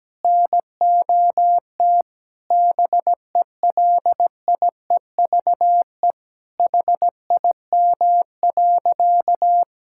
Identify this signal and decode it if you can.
Morse code, heard as a keyed tone, about 17 wpm, 700 Hz. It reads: NOT BELIEVE HIM.
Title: Sentences from Top 1000 Words - Review Part 5 - 17wpm